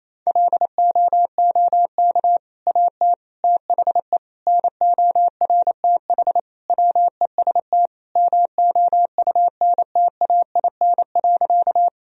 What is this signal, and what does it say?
Morse code, heard as a keyed tone, about 28 wpm, 700 Hz. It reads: LOOK AT THE NORTH WEST MOUNTAIN.
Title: Sentences from Top 1000 Words - Review Part 2 - 28wpm